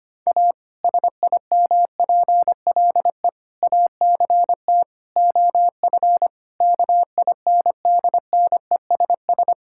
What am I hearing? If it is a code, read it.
Morse code, heard as a keyed tone, about 25 wpm, 700 Hz. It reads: A SIMPLE ACT OF KINDNESS
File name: practice / 25wpm / Difficult Sentences with Hints - 25wpm